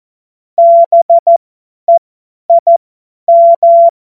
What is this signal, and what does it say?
Morse code, heard as a keyed tone, about 14 wpm, spaced as for 10 wpm, 680 Hz. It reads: BEIM
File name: Morsekurs / Witze / Witz 20. Januar 2026 50 F70